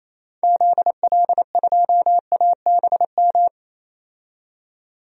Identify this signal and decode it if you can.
Morse code, heard as a keyed tone, about 28 wpm, 700 Hz. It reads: ZL2ABM
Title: DXCC-100 Call Signs - 6 Characters - 28wpm